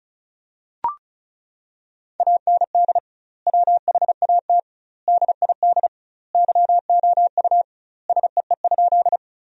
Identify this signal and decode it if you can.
Morse code, heard as a keyed tone, about 36 wpm, 700 Hz. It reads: AND WHAT DID YOU SEE?
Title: 2x Word-Spacing - Sentences from Top 100 Words - 35wpm